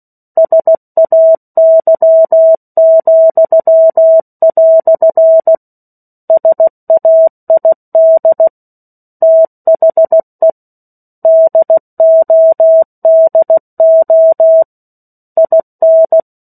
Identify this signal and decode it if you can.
Morse code, heard as a keyed tone, about 16 wpm, 650 Hz. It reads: SAY," SAID THE DODO IN